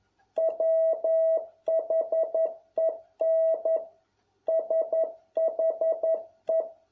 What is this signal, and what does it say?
Morse code, heard as a keyed tone, about 11 wpm, 650 Hz. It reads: WHEN SHE